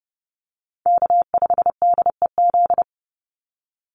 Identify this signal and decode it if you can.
Morse code, heard as a keyed tone, about 30 wpm, 700 Hz. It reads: K5DEZ
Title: NA SOTA Chaser Callsigns 30wpm